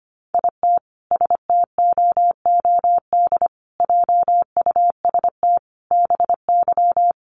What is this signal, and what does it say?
Morse code, heard as a keyed tone, about 25 wpm, 700 Hz. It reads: IT STOOD JUST BY
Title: Sentences from Top 1400 Words 25wpm